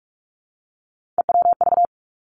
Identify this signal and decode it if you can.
Morse code, heard as a keyed tone, about 46 wpm, 700 Hz. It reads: EWV
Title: Three Challenging Letter-Number Combinations 45wpm